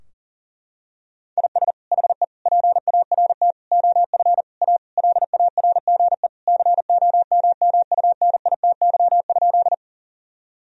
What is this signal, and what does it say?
Morse code, heard as a keyed tone, about 40 wpm, 700 Hz. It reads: IS HE PART OF A LARGE COMMUNITY?
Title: Sentences from Top 1000 Words - Review Part 4 - 40wpm